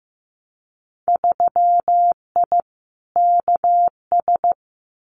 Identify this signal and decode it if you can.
Morse code, heard as a keyed tone, about 15 wpm, 700 Hz. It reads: 3I KS
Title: Winter Field Day Exchanges 15wpm